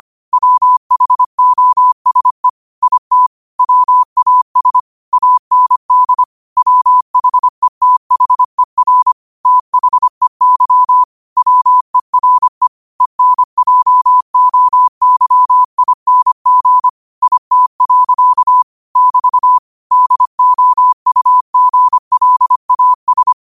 Morse code, heard as a keyed tone, about 25 wpm, 1000 Hz. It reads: WHOSE IT WAS AND WHETHER THEY WERE ENJOYING IT. <BT> DOUGLAS
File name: 25wpm